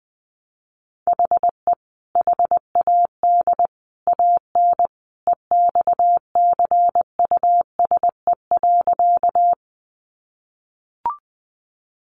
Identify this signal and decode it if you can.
Morse code, heard as a keyed tone, about 20 wpm, 700 Hz. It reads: HE HAD AN EXCUSE.
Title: Sentences from Top 800 Words 20wpm